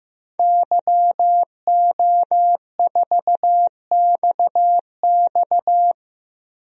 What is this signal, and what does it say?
Morse code, heard as a keyed tone, about 15 wpm, 700 Hz. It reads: YO4XX
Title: DXCC-100 Call Signs - 5 Characters - Repeated 3x - 15wpm